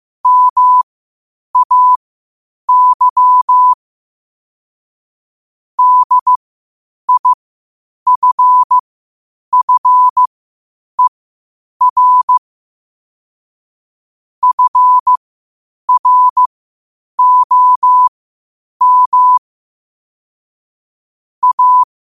Morse code, heard as a keyed tone, about 15 wpm, spaced as for 8 wpm, 1000 Hz. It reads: MAY DIFFER FROM A
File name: qotdfiles / 8wpm